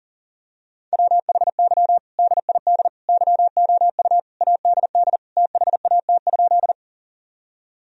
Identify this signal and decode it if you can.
Morse code, heard as a keyed tone, about 40 wpm, 700 Hz. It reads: WHY DID YOU ADD THAT?
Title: Sentences from Top 1000 Words - Review Part 1 - 40wpm